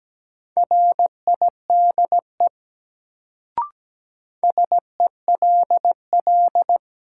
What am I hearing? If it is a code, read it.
Morse code, heard as a keyed tone, about 17 wpm, 700 Hz. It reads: RIDE SELL
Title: Sentences from Top 1000 Words - Review Part 6 - New Word Drill - 17wpm